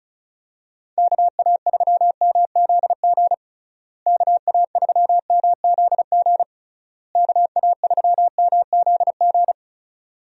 Morse code, heard as a keyed tone, about 36 wpm, 700 Hz. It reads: KA3MZG KA3MZG KA3MZG
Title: USA Call Signs - 6 Characters - Repeated 3x - 35wpm